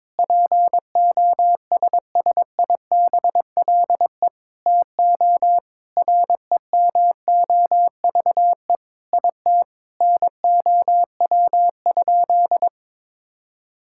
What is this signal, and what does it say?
Morse code, heard as a keyed tone, about 22 wpm, 700 Hz. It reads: POSSIBLE TO REMOVE IT NOW?
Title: Sentences from Top 1000 Words - Review Part 5 - 22wpm